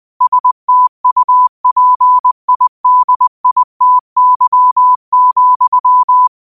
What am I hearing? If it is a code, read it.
Morse code, heard as a keyed tone, about 20 wpm, 1000 Hz. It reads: STUPIDITY,
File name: qotdfiles / 20wpm